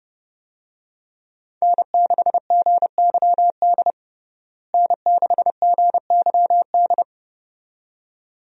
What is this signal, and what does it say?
Morse code, heard as a keyed tone, about 30 wpm, 700 Hz. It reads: N6GYD N6GYD
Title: USA Call Signs - 5 Characters - Repeated 2x - 30wpm